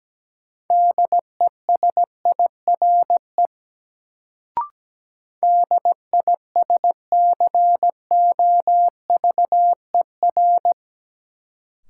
Morse code, heard as a keyed tone, about 17 wpm, 700 Hz. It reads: DESIRE DISCOVER
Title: Sentences from Top 1100 Words - New Word Drill - 17wpm